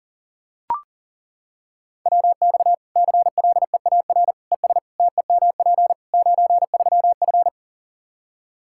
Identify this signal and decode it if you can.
Morse code, heard as a keyed tone, about 40 wpm, 700 Hz. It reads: WX CLEAR ES TEMP 93F
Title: K1USN SST QSO Practice 40wpm